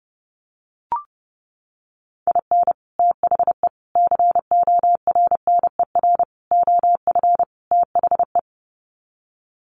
Morse code, heard as a keyed tone, about 30 wpm, 700 Hz. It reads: IN THE CORNER OF THE
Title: Sets of 5 Words Encore 30wpm